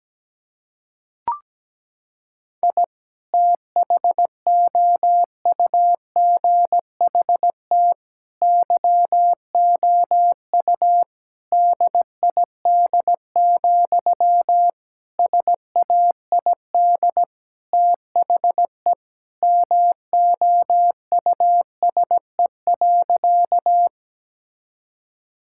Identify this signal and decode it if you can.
Morse code, heard as a keyed tone, about 17 wpm, 700 Hz. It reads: I THOUGHT YOU DID, SAID THE MOUSE.
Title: Sentences from Top 1000 Words - Review Part 5 - 17wpm